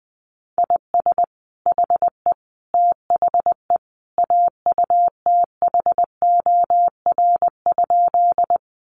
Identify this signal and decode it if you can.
Morse code, heard as a keyed tone, about 20 wpm, 700 Hz. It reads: IS HE THE AUTHOR?
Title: Sentences from Top 1200 Words 20wpm